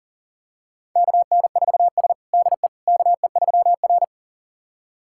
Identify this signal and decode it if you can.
Morse code, heard as a keyed tone, about 40 wpm, 700 Hz. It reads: KN4S DE KE3R
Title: USA Callsign DE Callsign - 4 Characters - 40wpm